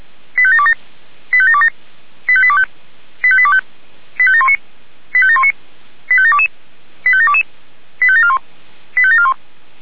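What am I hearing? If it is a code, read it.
Five-tone selective call (ZVEI), heard as quick runs of five tones, tones of about 70 ms. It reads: 85727 85727 85825 85825 85719 85719 85722 85722 85741 85741
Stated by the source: ZVEI five-tone selective call received by SDR.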